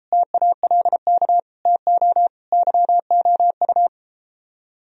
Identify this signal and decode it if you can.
Morse code, heard as a keyed tone, about 34 wpm, 700 Hz. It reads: TALK TO YOU